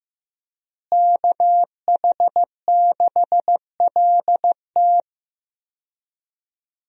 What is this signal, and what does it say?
Morse code, heard as a keyed tone, about 16 wpm, 700 Hz. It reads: KH6LT
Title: USA Call Signs - 5 Characters - Repeated 2x - 15wpm